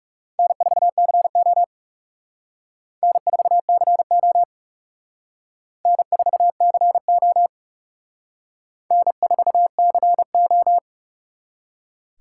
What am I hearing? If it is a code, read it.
Morse code, heard as a keyed tone, about 37 wpm, 700 Hz. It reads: N4CO N4CO N4CO N4CO
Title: Speed Racing - USA Call Signs 30wpm